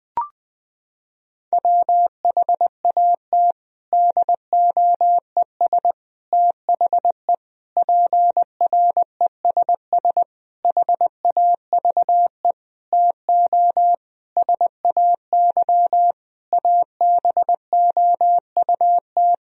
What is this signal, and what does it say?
Morse code, heard as a keyed tone, about 20 wpm, 700 Hz. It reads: WHAT DOES THE PRESS HAVE TO SAY ABOUT
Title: Sentences from Top 1000 Words - Review Part 3 - 20wpm